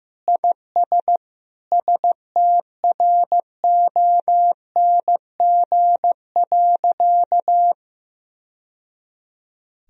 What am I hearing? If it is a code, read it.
Morse code, heard as a keyed tone, about 16 wpm, 700 Hz. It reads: IS STRONG.
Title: Sentences from Top 1000 Words - Review Part 4 - 15wpm